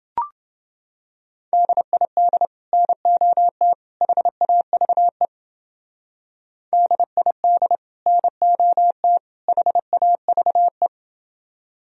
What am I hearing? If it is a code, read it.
Morse code, heard as a keyed tone, about 29 wpm, 700 Hz. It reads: DID NOT HAVE DID NOT HAVE